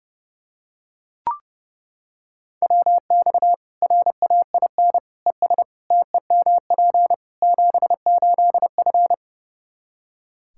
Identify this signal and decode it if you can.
Morse code, heard as a keyed tone, about 32 wpm, 700 Hz. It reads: WX RAIN ES TEMP 78F
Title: K1USN SST QSO Practice 30wpm